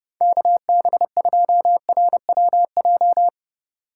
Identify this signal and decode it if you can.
Morse code, heard as a keyed tone, about 30 wpm, 700 Hz. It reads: KB2RWJ
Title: USA Call Signs - 6 Characters - 3x Character Spacing - 30wpm